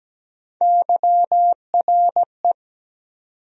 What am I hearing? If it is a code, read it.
Morse code, heard as a keyed tone, about 17 wpm, 700 Hz. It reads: YRE
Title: USA Call Sign Postfixes - 3 Letters 17wpm